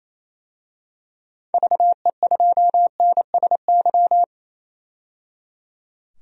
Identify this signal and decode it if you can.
Morse code, heard as a keyed tone, about 28 wpm, 700 Hz. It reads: VE2NSY
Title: DXCC-100 Call Signs - 6 Characters - 28wpm